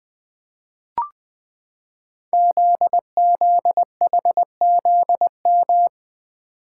Morse code, heard as a keyed tone, about 20 wpm, 700 Hz. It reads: ZZHZM
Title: Code Groups - 5 Letters - 20wpm